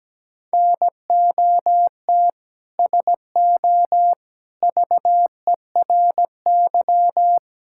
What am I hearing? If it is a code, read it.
Morse code, heard as a keyed tone, about 17 wpm, 700 Hz. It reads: NOT SO VERY
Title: Sentences from Top 1600 Words 17wpm